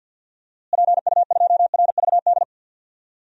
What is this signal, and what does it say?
Morse code, heard as a keyed tone, about 52 wpm, 700 Hz. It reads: PU2RVD
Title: DXCC-100 Call Signs - 6 Characters - Repeated 2x - 50wpm